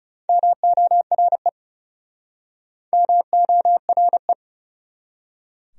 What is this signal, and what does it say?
Morse code, heard as a keyed tone, about 37 wpm, 700 Hz. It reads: MORE MORE
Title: Speed-Racing Top 100 Words 30wpm